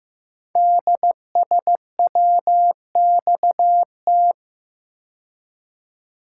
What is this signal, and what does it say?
Morse code, heard as a keyed tone, about 15 wpm, 700 Hz. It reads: DSWXT
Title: Code Groups - 5 Letters - 15wpm